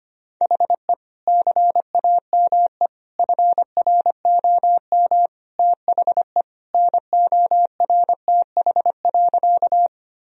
Morse code, heard as a keyed tone, about 25 wpm, 700 Hz. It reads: HE CAME FROM THE NORTH.
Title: Sentences from Top 1000 Words - Review Part 1 - 25wpm